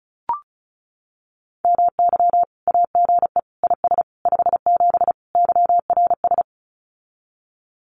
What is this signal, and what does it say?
Morse code, heard as a keyed tone, about 34 wpm, 700 Hz. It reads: MY AGE IS 57 YRS
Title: Common QSO Phrases Encore 35wpm